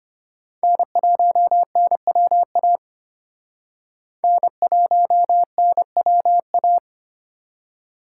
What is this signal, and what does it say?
Morse code, heard as a keyed tone, about 31 wpm, 700 Hz. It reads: N1NWA N1NWA
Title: Speed Racing - USA Call Signs 25wpm